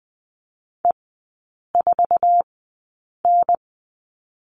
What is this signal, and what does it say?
Morse code, heard as a keyed tone, about 20 wpm, spaced as for 8 wpm, 700 Hz. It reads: E4N
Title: 060 Lesson 11e - Letter h - Three Characters 20wpm